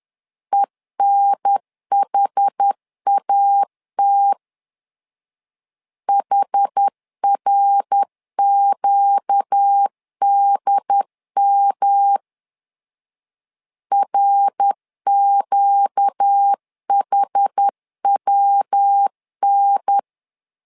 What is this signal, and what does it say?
Morse code, heard as a keyed tone, about 10 wpm, 800 Hz. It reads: ENHAT HRQDM RQHWN
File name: LEZIONI / 4 Q 10wpm 2210